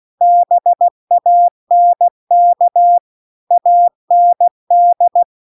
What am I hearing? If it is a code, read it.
Morse code, heard as a keyed tone, about 16 wpm, 700 Hz. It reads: BANK AND